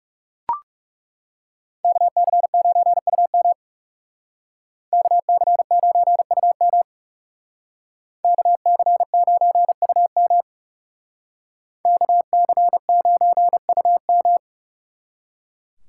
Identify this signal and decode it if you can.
Morse code, heard as a keyed tone, about 37 wpm, 700 Hz. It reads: KC9UM KC9UM KC9UM KC9UM
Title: Speed Racing - USA 2x2 Call Signs 30wpm